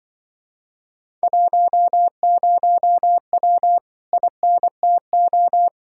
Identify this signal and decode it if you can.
Morse code, heard as a keyed tone, about 24 wpm, 700 Hz. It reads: 10W INTO
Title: Speed-Racing Common QSO Phrases - 20wpm